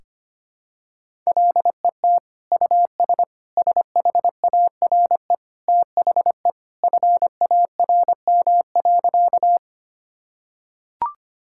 Morse code, heard as a keyed tone, about 25 wpm, 700 Hz. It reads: LET US SHARE THE FARM.
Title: Sentences from Top 1000 Words - Review Part 6 - 25wpm